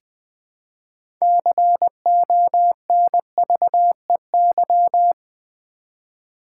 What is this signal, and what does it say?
Morse code, heard as a keyed tone, about 20 wpm, 700 Hz. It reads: CONVEY
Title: Build A Word - 2 Letters At A Time - 6-Letter Words - 20wpm